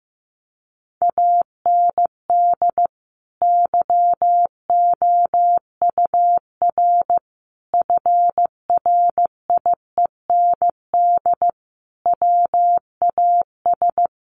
Morse code, heard as a keyed tone, about 15 wpm, 700 Hz. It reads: AND YOUR FRIEND WAS
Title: Sentences from Top 1400 Words 15wpm